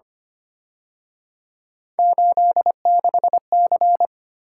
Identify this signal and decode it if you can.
Morse code, heard as a keyed tone, about 29 wpm, 700 Hz. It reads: 86C
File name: Speed-Racing - Three Letter-Number Combinations - 25wpm